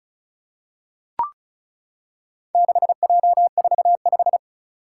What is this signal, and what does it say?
Morse code, heard as a keyed tone, about 36 wpm, 700 Hz. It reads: 6J45
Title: Four Challenging Letter-Number Combinations 35wpm